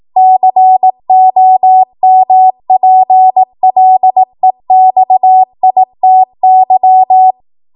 Morse code, heard as a keyed tone, about 18 wpm, 750 Hz. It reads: COMPLEXITY
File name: Morse / 18WPM